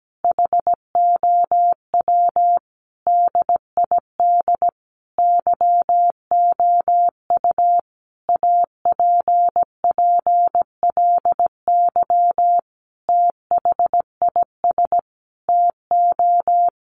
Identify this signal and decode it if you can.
Morse code, heard as a keyed tone, about 17 wpm, 700 Hz. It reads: HOW DID YOU APPLY THIS TO
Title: Sentences from Top 1200 Words 17wpm